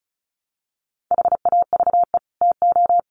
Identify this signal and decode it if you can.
Morse code, heard as a keyed tone, about 34 wpm, 700 Hz. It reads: HAVE TO
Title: Sets of 5 Words Encore 35wpm